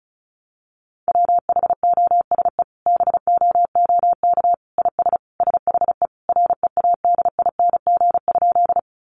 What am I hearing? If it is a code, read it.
Morse code, heard as a keyed tone, about 34 wpm, 700 Hz. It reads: WHOSE BOOK IS SHE READING?
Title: Sentences from Top 1200 Words 35wpm